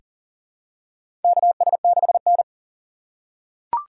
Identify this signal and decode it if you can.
Morse code, heard as a keyed tone, about 40 wpm, 700 Hz. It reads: KS6N
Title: USA Call Signs - 8x Character Spacing - 40wpm